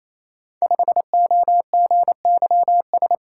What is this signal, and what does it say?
Morse code, heard as a keyed tone, about 28 wpm, 700 Hz. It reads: 5OGYS
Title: Code Groups - 5 Letter:Numbers - 28wpm